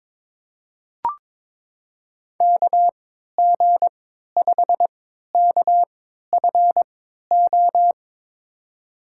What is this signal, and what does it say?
Morse code, heard as a keyed tone, about 22 wpm, spaced as for 13 wpm, 700 Hz. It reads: KG5KFO
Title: USA Call Signs - 3x Character Spacing - 22wpm